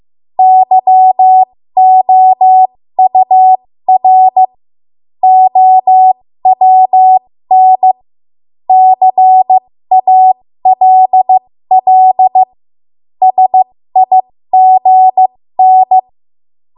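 Morse code, heard as a keyed tone, about 15 wpm, spaced as for 13 wpm, 750 Hz. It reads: YOUR OWN CALL SIGN
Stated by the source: ARRL Morse code practice at 13 WPM.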